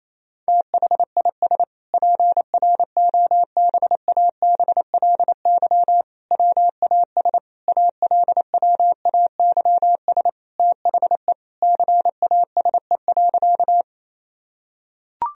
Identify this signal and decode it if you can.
Morse code, heard as a keyed tone, about 28 wpm, 700 Hz. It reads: THIS PROBABLY WAS ALWAYS THE CASE.
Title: Sentences from Top 1000 Words - Review Part 4 - 28wpm